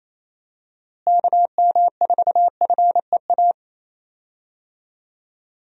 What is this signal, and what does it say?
Morse code, heard as a keyed tone, about 28 wpm, 700 Hz. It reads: KM4FEA
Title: USA Call Signs - 6 Characters - Repeated 3x - 28wpm